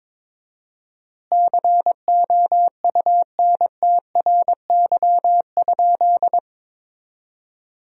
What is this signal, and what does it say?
Morse code, heard as a keyed tone, about 22 wpm, 700 Hz. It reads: COUNTRY?
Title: 4x Word-Spacing - Sentences from Top 300 Words - 22wpm